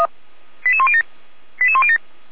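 Five-tone selective call (ZVEI), heard as quick runs of five tones, tones of about 70 ms. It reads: 88107 88107
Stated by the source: ZVEI five-tone selective call received by SDR.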